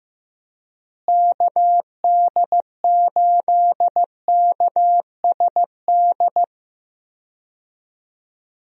Morse code, heard as a keyed tone, about 15 wpm, 700 Hz. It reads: KD8KSD
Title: USA Call Signs - 6 Characters - Repeated 2x - 15wpm